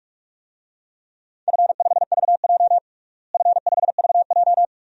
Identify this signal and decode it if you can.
Morse code, heard as a keyed tone, about 46 wpm, 700 Hz. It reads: F5VJ F5VJ
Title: DXCC-100 Call Signs - 4 Characters - Repeated 2x - 45wpm